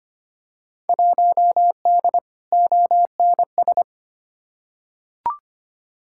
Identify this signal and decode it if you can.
Morse code, heard as a keyed tone, about 25 wpm, 700 Hz. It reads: 1D ONS
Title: Field Day Exchanges - 25wpm